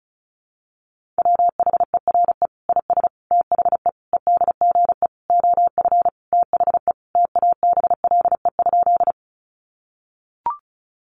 Morse code, heard as a keyed tone, about 34 wpm, 700 Hz. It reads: WHERE IS THE EDGE OF THE TABLE?
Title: Sentences from Top 1300 Words 35wpm